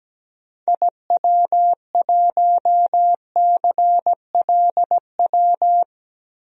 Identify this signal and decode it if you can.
Morse code, heard as a keyed tone, about 17 wpm, 700 Hz. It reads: IW1CLW
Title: DXCC-100 Call Signs - 6 Characters - 17wpm